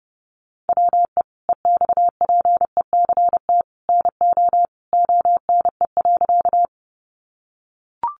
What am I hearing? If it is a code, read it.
Morse code, heard as a keyed tone, about 28 wpm, 700 Hz. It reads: WE EXPECT NO ONE.
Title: Sentences from Top 1100 Words 30wpm